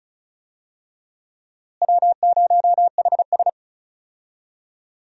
Morse code, heard as a keyed tone, about 36 wpm, 700 Hz. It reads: W0HS
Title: USA Call Signs - 4 Characters - Repeated 2x - 35wpm